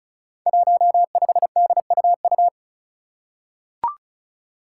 Morse code, heard as a keyed tone, about 36 wpm, 700 Hz. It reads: 15DUU
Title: Five Challenging Letter-Number Combinations 35wpm